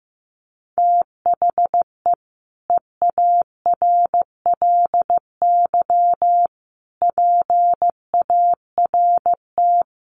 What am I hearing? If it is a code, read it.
Morse code, heard as a keyed tone, about 15 wpm, 700 Hz. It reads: THE EARLY PART
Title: Sets of 5 Words Encore2 15wpm